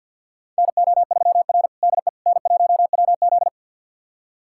Morse code, heard as a keyed tone, about 47 wpm, 700 Hz. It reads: NO3R DE N1WZ
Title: USA Callsign DE Callsign - 4 Characters - 50wpm